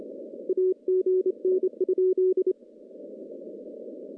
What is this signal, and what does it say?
Morse code, heard as a keyed tone, about 26 wpm, 410 Hz. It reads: AGN?
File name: contests / CW-SS-k5tr- -0200